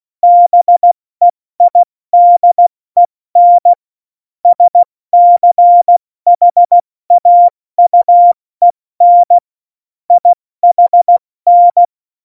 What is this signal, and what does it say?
Morse code, heard as a keyed tone, about 16 wpm, 700 Hz. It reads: BEIDEN SCHAUEN IHN
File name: Morsekurs / Witze / Witz 08. Oktober 2019, 70 F80